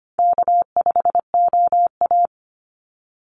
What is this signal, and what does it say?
Morse code, heard as a keyed tone, about 25 wpm, 700 Hz. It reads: K5OA
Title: Call Signs - 25wpm